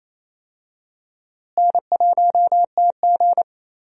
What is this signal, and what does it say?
Morse code, heard as a keyed tone, about 28 wpm, 700 Hz. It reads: N1TG
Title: USA Call Signs - 4 Characters - Repeated 3x - 28wpm